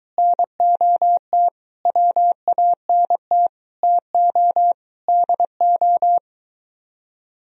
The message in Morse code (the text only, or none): NOT WANT TO DO